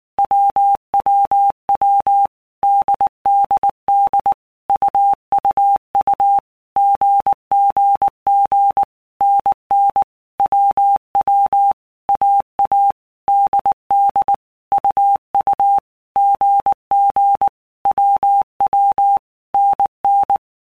Morse code, heard as a keyed tone, about 19 wpm, 800 Hz. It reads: WWW DDD UUU GGG NN WW AA DD UU GG WW NN
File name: Basic Letter Groups By Blocks / Groups@18 / AWUNDG@18